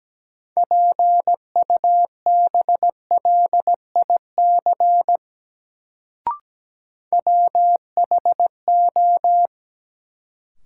Morse code, heard as a keyed tone, about 17 wpm, 700 Hz. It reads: PUBLIC WHO
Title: Sets of 2 Words - New Word Drill - 17wpm